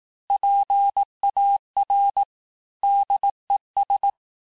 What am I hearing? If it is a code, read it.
Morse code, heard as a keyed tone, about 18 wpm, 800 Hz. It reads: PAR DES
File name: cours-du-lundi files / cours-1588-FIN-de-QSO-vitesse-17MPM